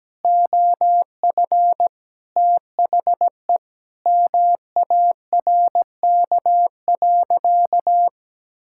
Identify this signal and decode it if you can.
Morse code, heard as a keyed tone, about 17 wpm, 700 Hz. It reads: OF THE MARK.
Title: Sentences from Top 1000 Words - Review Part 6 - 17wpm